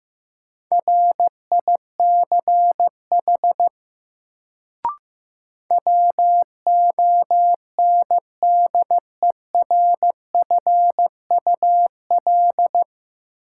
Morse code, heard as a keyed tone, about 15 wpm, 700 Hz. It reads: RICH WONDERFUL
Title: Sentences from Top 1100 Words - New Word Drill - 15wpm